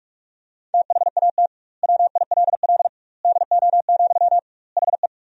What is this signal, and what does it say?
Morse code, heard as a keyed tone, about 47 wpm, 700 Hz. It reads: THAT WILL DO, HE
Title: 2x Word-Spacing - Sentences from Top 100 Words - 45wpm